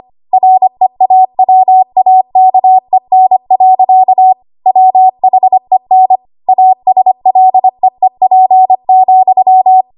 Morse code, heard as a keyed tone, about 25 wpm, 750 Hz. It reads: REAWAKEN. WHEN ASLEEP,
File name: Archive / 25 WPM / 25WPM